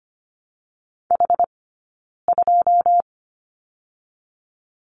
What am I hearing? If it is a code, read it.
Morse code, heard as a keyed tone, about 24 wpm, 700 Hz. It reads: H 2
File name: course / 25:10wpm / 092 Lesson 16d - Letter c - Two Characters 25wpm